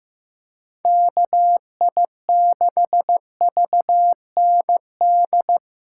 Morse code, heard as a keyed tone, about 18 wpm, 700 Hz. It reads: KI6VND
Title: Speed Racing - USA 2x3 Call Signs 15wpm